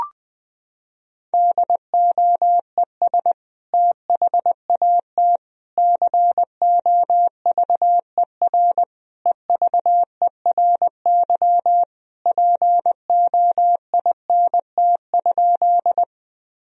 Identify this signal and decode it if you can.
Morse code, heard as a keyed tone, about 20 wpm, 700 Hz. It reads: DOES THAT COVER EVERY POINT?
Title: Sentences from Top 1000 Words - Review Part 1 - 20wpm